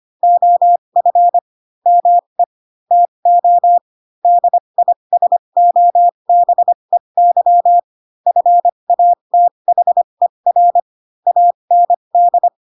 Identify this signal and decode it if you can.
Morse code, heard as a keyed tone, about 25 wpm, spaced as for 22 wpm, 700 Hz. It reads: OF ME TO DISOBEY FATHER AND